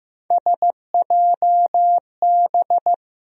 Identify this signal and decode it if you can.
Morse code, heard as a keyed tone, about 15 wpm, 700 Hz. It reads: SJB
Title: USA Call Sign Postfixes - 3 Letters 15wpm